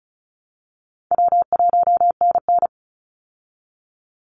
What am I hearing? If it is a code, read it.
Morse code, heard as a keyed tone, about 34 wpm, 700 Hz. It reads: W1NN
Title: Call Signs Repeated 3x - 35wpm